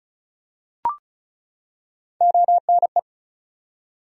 Morse code, heard as a keyed tone, about 36 wpm, 700 Hz. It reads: ONE
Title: Top 100 Words 35wpm